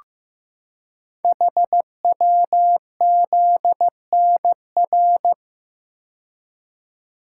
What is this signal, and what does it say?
Morse code, heard as a keyed tone, about 15 wpm, 700 Hz. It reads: HWZNR
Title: Code Groups - 5 Letter:Numbers - 15wpm